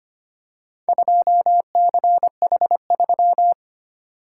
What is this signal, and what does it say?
Morse code, heard as a keyed tone, about 25 wpm, 700 Hz. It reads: 2CH3
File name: practice / 25wpm / Four Letter-Number Combinations - 25wpm